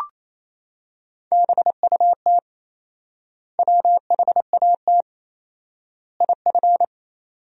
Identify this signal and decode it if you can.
Morse code, heard as a keyed tone, about 27 wpm, 700 Hz. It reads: BUT WHAT IF
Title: 4x Word-Spacing - Sentences from Top 100 Words - 28wpm